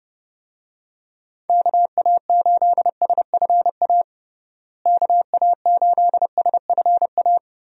Morse code, heard as a keyed tone, about 30 wpm, 700 Hz. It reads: KA8SFA KA8SFA
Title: USA Call Signs - 6 Characters - Repeated 2x - 30wpm